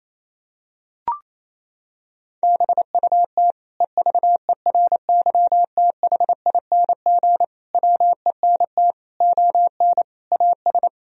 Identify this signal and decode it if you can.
Morse code, heard as a keyed tone, about 28 wpm, 700 Hz. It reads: BUT EVERYTHING WENT ON AS
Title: Sentences from Top 1000 Words - Review Part 2 - 28wpm